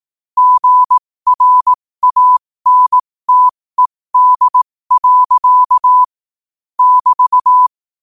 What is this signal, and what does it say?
Morse code, heard as a keyed tone, about 18 wpm, spaced as for 15 wpm, 1000 Hz. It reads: GRANTED. <BT>
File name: qotdfiles / 15wpm